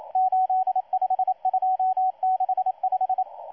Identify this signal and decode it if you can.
Morse code, heard as a keyed tone, about 27 wpm, 750 Hz, with radio noise. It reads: 85265
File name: M12-2016-07-07-1310utc-13926khz-error-restart